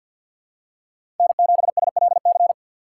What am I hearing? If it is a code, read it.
Morse code, heard as a keyed tone, about 49 wpm, 700 Hz. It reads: N7SLC
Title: USA Call Signs - 4x Character Spacing - 50wpm